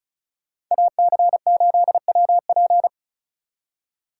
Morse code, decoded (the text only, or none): AC8WP